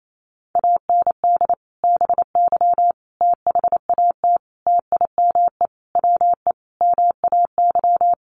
Morse code, heard as a keyed tone, about 28 wpm, 700 Hz. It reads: AND BY THAT TIME WE MAY
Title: Sentences from Top 1100 Words 28wpm